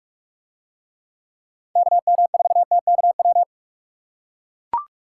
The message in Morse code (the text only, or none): KM4TKW